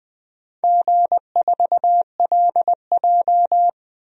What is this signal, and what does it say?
Morse code, heard as a keyed tone, about 21 wpm, 700 Hz. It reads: G4LJ
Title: DXCC-100 Call Signs - 4 Characters - Repeated 3x - 20wpm